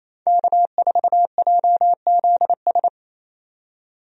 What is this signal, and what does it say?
Morse code, heard as a keyed tone, about 27 wpm, 700 Hz. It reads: K4JZS
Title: USA Call Signs - 5 Characters - 28wpm